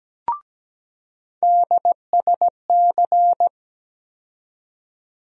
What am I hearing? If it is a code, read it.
Morse code, heard as a keyed tone, about 17 wpm, 700 Hz. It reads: DSC
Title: USA Call Sign Postfixes - 3 Letters 17wpm